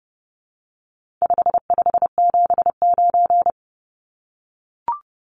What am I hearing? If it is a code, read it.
Morse code, heard as a keyed tone, about 30 wpm, 700 Hz. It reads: 5579
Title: Numbers - 4-Digit 30wpm